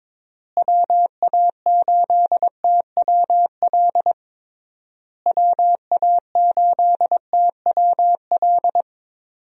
Morse code, heard as a keyed tone, about 22 wpm, 700 Hz. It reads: WA8TWL WA8TWL
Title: USA Call Signs - 6 Characters - Repeated 3x - 22wpm